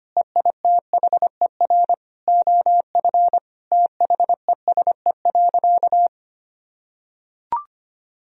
Morse code, heard as a keyed tone, about 25 wpm, 700 Hz. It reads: EITHER OF THESE.
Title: Sentences from Top 1000 Words - Review Part 4 - 25wpm